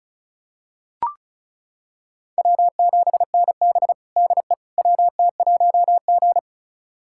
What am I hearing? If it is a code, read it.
Morse code, heard as a keyed tone, about 34 wpm, 700 Hz. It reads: W7NB DE WT1G
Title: USA Callsign DE Callsign - 4 Characters - 35wpm